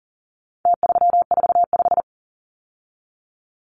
Morse code, heard as a keyed tone, about 40 wpm, 700 Hz. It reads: T345
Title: Numbers - Contest Style - 4-Digit 40wpm